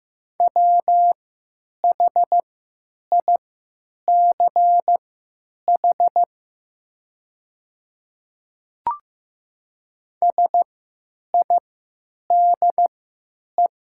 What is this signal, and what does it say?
Morse code, heard as a keyed tone, about 15 wpm, spaced as for 9 wpm, 700 Hz. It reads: WHICH SIDE
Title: 3x Character Spacing - Top 100 Words - 15wpm